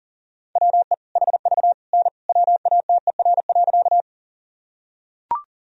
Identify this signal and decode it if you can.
Morse code, heard as a keyed tone, about 40 wpm, 700 Hz. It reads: WE HV N WATER.
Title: Sentences from Top 100 Words and Abbreviations 40wpm